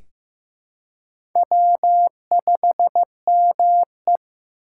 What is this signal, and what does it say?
Morse code, heard as a keyed tone, about 15 wpm, 700 Hz. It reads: W5ME
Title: USA Call Signs - 4 Characters - 3x Character Spacing - 15wpm